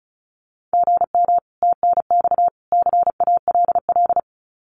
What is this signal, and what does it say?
Morse code, heard as a keyed tone, about 34 wpm, 700 Hz. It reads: GM TNX CALL
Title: Common QSO Phrases 35wpm